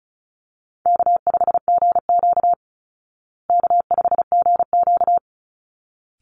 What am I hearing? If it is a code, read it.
Morse code, heard as a keyed tone, about 34 wpm, 700 Hz. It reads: K5GQ K5GQ
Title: Easy Call Signs Repeated 2x - 35wpm